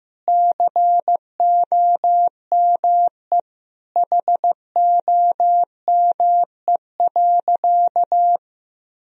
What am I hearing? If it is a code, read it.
Morse code, heard as a keyed tone, about 15 wpm, 700 Hz. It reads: COME HOME.
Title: Sentences from Top 1000 Words - Review Part 3 - 15wpm